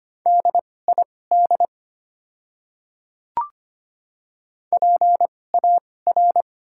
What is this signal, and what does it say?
Morse code, heard as a keyed tone, about 25 wpm, spaced as for 18 wpm, 700 Hz. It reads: DID PAR
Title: 2x Character Spacing - Top 200 Words - 25wpm